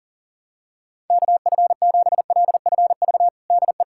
Morse code, heard as a keyed tone, about 37 wpm, 700 Hz. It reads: KF7LFV DE
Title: USA Callsign DE Callsign 40wpm